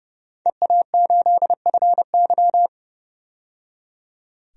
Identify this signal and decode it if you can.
Morse code, heard as a keyed tone, about 31 wpm, 700 Hz. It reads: EA8FY
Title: DXCC-100 Call Signs - 5 Characters - 30wpm